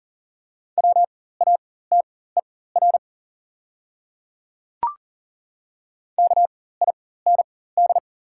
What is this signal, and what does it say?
Morse code, heard as a keyed tone, about 40 wpm, spaced as for 19 wpm, 700 Hz. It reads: WATER KIND
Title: 4x Character Spacing - Top 200 Words - 40wpm